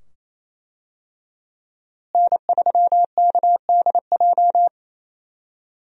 Morse code, heard as a keyed tone, about 28 wpm, 700 Hz. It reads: N3KDJ
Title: USA Call Signs - 5 Characters - Repeated 2x - 28wpm